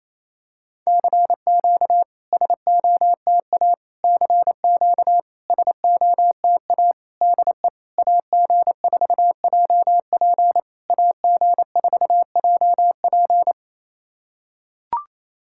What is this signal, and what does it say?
Morse code, heard as a keyed tone, about 28 wpm, 700 Hz. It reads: CQ SOTA CQ SOTA DE AG4JP AG4JP
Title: SOTA QSO Practice 28wpm